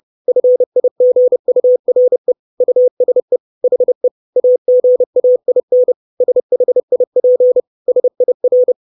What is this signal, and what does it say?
Morse code, heard as a keyed tone, about 30 wpm, 500 Hz. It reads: FIGURE USE HE AGAIN SHIP SIR
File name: 632 Words 30WPM